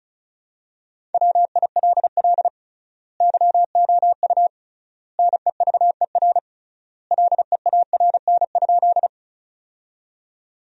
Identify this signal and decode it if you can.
Morse code, heard as a keyed tone, about 36 wpm, 700 Hz. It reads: WILL YOU NEVER LEARN?
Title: 3x Word-Spacing - Sentences from Top 300 Words - 35wpm